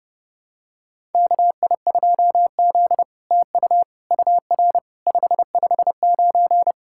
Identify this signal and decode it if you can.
Morse code, heard as a keyed tone, about 28 wpm, 700 Hz. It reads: KI2Z TU UR 559
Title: SOTA QSO Practice 30wpm